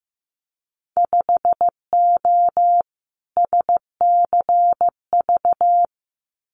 Morse code, heard as a keyed tone, about 15 wpm, 700 Hz. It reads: 5O SCV
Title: Winter Field Day Exchanges Encore 15wpm